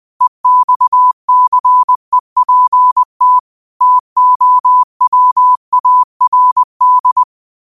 Morse code, heard as a keyed tone, about 20 wpm, 1000 Hz. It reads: EXCEPT TOWARD